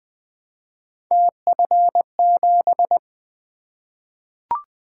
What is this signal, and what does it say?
Morse code, heard as a keyed tone, about 20 wpm, 700 Hz. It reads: TF7
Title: Three Letter-Number Combinations - 20wpm